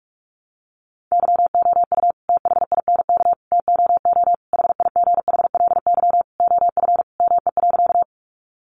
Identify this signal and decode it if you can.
Morse code, heard as a keyed tone, about 44 wpm, 700 Hz. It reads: YOU THINK TOO HIGHLY OF ME.
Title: Sentences from Top 1100 Words 45wpm